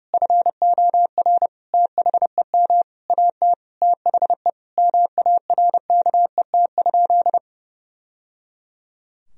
Morse code, heard as a keyed tone, about 32 wpm, 700 Hz. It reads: FOR THEM AT THE MARKET?
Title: Sentences from Top 1000 Words - Review Part 3 - 30wpm